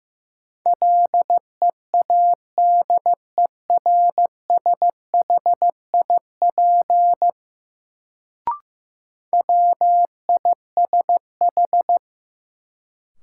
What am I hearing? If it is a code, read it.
Morse code, heard as a keyed tone, about 15 wpm, 700 Hz. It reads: LEADERSHIP WISH
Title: Sentences from Top 1600 Words - New Word Drill Alt - 15wpm